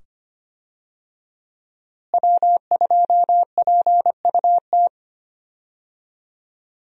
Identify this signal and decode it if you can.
Morse code, heard as a keyed tone, about 25 wpm, 700 Hz. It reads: W2PUT
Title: USA Call Signs - 5 Characters - Repeated 2x - 25wpm